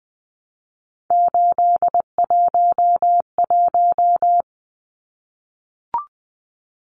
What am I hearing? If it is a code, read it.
Morse code, heard as a keyed tone, about 20 wpm, 700 Hz. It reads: 811
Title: Numbers - 3-Digit 20wpm